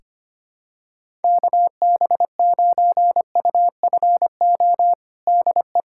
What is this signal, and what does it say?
Morse code, heard as a keyed tone, about 25 wpm, 700 Hz. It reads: KB9UFO DE
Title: USA Callsign DE Callsign - 6 Characters - 25wpm